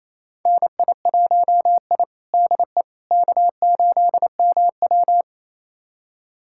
Morse code, heard as a keyed tone, about 28 wpm, 700 Hz. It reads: NI1I DE K8MW
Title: USA Callsign DE Callsign - 4 Characters - 28wpm